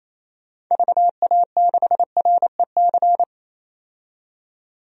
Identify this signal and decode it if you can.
Morse code, heard as a keyed tone, about 28 wpm, 700 Hz. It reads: VA6REC
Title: DXCC-100 Call Signs - 6 Characters - 28wpm